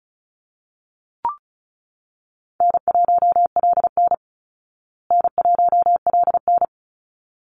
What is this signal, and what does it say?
Morse code, heard as a keyed tone, about 34 wpm, 700 Hz. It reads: N1LN N1LN
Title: Easy Call Signs Repeated 2x - 35wpm